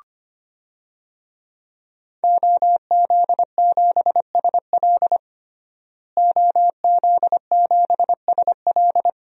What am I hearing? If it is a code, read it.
Morse code, heard as a keyed tone, about 25 wpm, 700 Hz. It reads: OZ7SL OZ7SL
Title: DXCC-100 Call Signs - 5 Characters - Repeated 2x - 25wpm